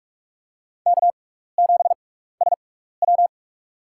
Morse code, heard as a keyed tone, about 45 wpm, spaced as for 15 wpm, 700 Hz. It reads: K7SW
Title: USA Call Signs - 4 Characters - 6x Character Spacing - 45wpm